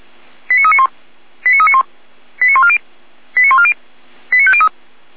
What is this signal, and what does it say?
Five-tone selective call (ZVEI), heard as quick runs of five tones, tones of about 70 ms. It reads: 89391 89391 89140 89140 89503